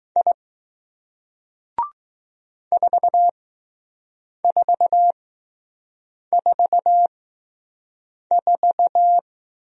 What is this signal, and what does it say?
Morse code, heard as a keyed tone, about 20 wpm, spaced as for 6 wpm, 700 Hz. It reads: I 4444